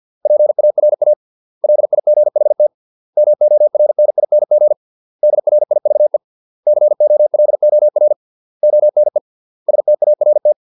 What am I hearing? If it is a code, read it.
Morse code, heard as a keyed tone, about 50 wpm, 600 Hz. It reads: PARA LIGHT MORNING DRIVE COLOR ONE START
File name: 632 words / 632 Words 50WPM